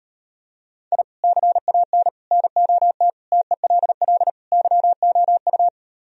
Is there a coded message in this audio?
Yes: Morse code from 0.9 to 5.7 s, heard as a keyed tone, about 38 wpm, 700 Hz. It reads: I CAN NOT TELL YOU